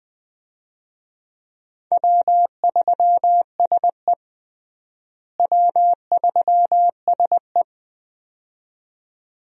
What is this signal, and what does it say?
Morse code, heard as a keyed tone, about 20 wpm, 700 Hz. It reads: W3SE W3SE
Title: USA Call Signs - 4 Characters - Repeated 2x - 20wpm